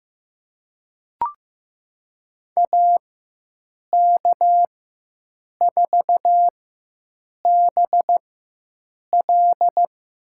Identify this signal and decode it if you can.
Morse code, heard as a keyed tone, about 15 wpm, spaced as for 7 wpm, 700 Hz. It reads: AK4BL
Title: USA Call Signs - 5 Characters - 4x Character Spacing - 15wpm